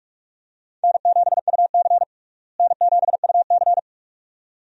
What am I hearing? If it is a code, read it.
Morse code, heard as a keyed tone, about 41 wpm, 700 Hz. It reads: N7UC N7UC
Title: USA Call Signs - 4 Characters - Repeated 2x - 45wpm